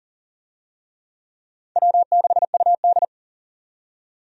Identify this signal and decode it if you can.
Morse code, heard as a keyed tone, about 40 wpm, 700 Hz. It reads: W6UD
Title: USA Call Signs - 4 Characters - 40wpm